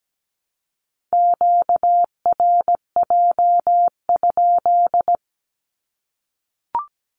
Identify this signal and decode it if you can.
Morse code, heard as a keyed tone, about 17 wpm, 700 Hz. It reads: QRJ?
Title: Q-Codes Repeated 3x 17wpm